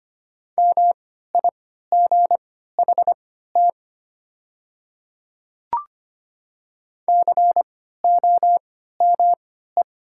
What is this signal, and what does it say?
Morse code, heard as a keyed tone, about 25 wpm, spaced as for 14 wpm, 690 Hz. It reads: MIGHT COME
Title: 3x Character Spacing - Top 500 Words - 25wpm